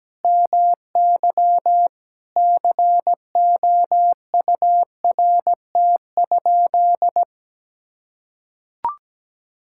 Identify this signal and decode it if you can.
Morse code, heard as a keyed tone, about 17 wpm, 700 Hz. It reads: MY COURT?
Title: Sentences from Top 1600 Words 17wpm